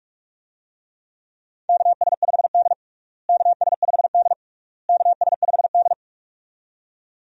Morse code, heard as a keyed tone, about 46 wpm, 700 Hz. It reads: KS5D KS5D KS5D
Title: USA Call Signs - 4 Characters - Repeated 3x - 45wpm